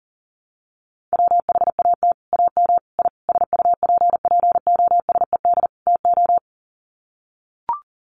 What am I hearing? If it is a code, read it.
Morse code, heard as a keyed tone, about 40 wpm, 700 Hz. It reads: WHAT AM I SUPPOSED TO
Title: Sets of 5 Words Encore2 40wpm